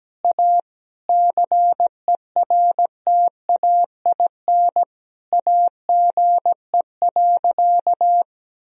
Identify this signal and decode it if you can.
Morse code, heard as a keyed tone, about 17 wpm, 700 Hz. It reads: A CERTAIN AGE.
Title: Sentences from Top 1000 Words - Review Part 3 - 17wpm